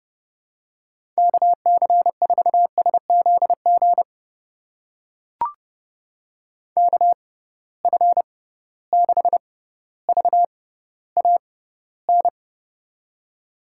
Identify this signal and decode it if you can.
Morse code, heard as a keyed tone, about 32 wpm, spaced as for 16 wpm, 700 Hz. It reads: KC4SZG KF6VAN
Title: USA Call Signs - 6 Characters - 6x Character Spacing - 30wpm